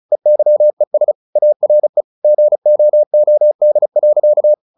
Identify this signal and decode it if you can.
Morse code, heard as a keyed tone, about 35 wpm, 600 Hz. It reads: EYES ARE GOOD.
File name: the old man and the sea / the old man and the sea 1 35wpm